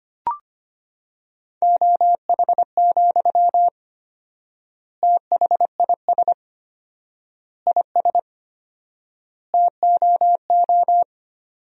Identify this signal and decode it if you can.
Morse code, heard as a keyed tone, about 25 wpm, 700 Hz. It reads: OH, THIS IS TOO
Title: 4x Word-Spacing - Sentences from Top 500 Words - 25wpm